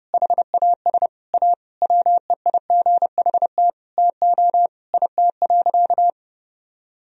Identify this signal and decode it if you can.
Morse code, heard as a keyed tone, about 30 wpm, 700 Hz. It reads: HAS A WEIGHT TO IT.
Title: Sentences from Top 1000 Words - Review Part 3 - 30wpm